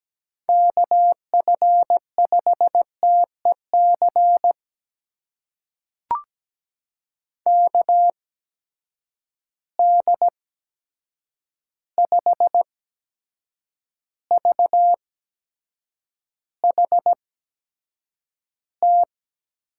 Morse code, heard as a keyed tone, about 17 wpm, spaced as for 7 wpm, 700 Hz. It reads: KF5TEC KD5VHT
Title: USA Call Signs - 6 Characters - 8x Character Spacing - 17wpm